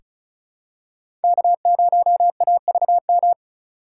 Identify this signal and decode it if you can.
Morse code, heard as a keyed tone, about 34 wpm, 700 Hz. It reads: K0AVM
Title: USA Call Signs - 5 Characters - 4x Character Spacing - 35wpm